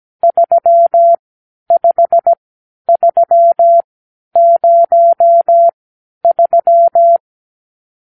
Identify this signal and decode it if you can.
Morse code, heard as a keyed tone, about 17 wpm, spaced as for 10 wpm, 680 Hz. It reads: 35303
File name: arrl / Review (1-0)